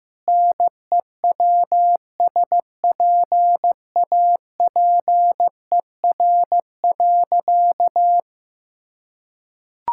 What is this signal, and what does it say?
Morse code, heard as a keyed tone, about 15 wpm, 700 Hz. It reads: NEWSPAPER.
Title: Sentences from Top 1000 Words - Review Part 5 - 15wpm